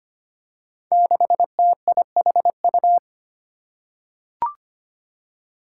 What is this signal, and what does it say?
Morse code, heard as a keyed tone, about 25 wpm, 700 Hz. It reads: 6TIHU
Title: Five Challenging Letter-Number Combinations 25wpm